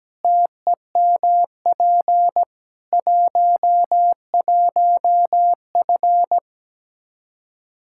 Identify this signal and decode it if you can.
Morse code, heard as a keyed tone, about 17 wpm, 700 Hz. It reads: TEMP 11F
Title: Practice QSO - Exchange 2 - Info - 17wpm